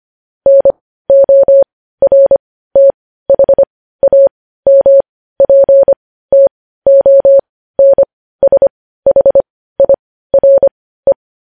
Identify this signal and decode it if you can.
Morse code, heard as a keyed tone, about 25 wpm, spaced as for 15 wpm, 550 Hz. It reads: NORTHAMPTONSHIRE
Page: News Headlines in Morse Code at 15 WPM